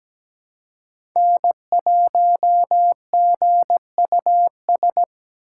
Morse code, heard as a keyed tone, about 20 wpm, 700 Hz. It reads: N1GUS
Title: Speed Racing - USA 1x3 Call Signs 17wpm